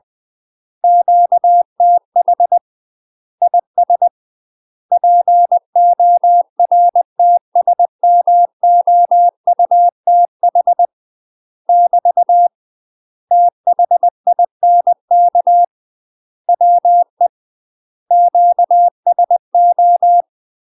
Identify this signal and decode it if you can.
Morse code, heard as a keyed tone, about 20 wpm, 700 Hz. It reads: QTH IS PORTSMOUTH <BT> THINK WE QSO
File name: UK MASTER QSOs TRAINING / 20WPM / UK MASTER QSOs 0040